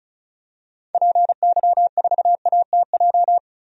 Morse code, heard as a keyed tone, about 34 wpm, 700 Hz. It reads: PY4ATJ
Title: DXCC-100 Call Signs - 6 Characters - 35wpm